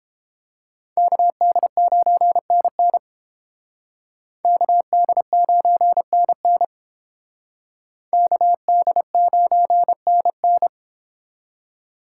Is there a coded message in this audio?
Yes: Morse code from 1.0 to 10.7 s, heard as a keyed tone, about 30 wpm, 700 Hz. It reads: KD9NN KD9NN KD9NN